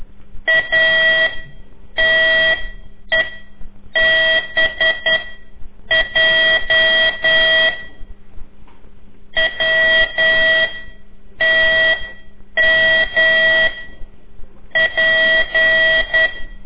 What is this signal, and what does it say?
Morse code, heard as a keyed tone, about 10 wpm, spaced as for 7 wpm, 1950 Hz. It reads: ATEBJ W TM P
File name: CORSI CW